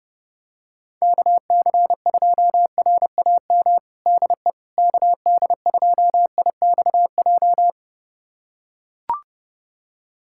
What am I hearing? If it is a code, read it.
Morse code, heard as a keyed tone, about 30 wpm, 700 Hz. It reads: KC2RAM DE KD2IXJ
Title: USA Callsign DE Callsign - 6 Characters - 30wpm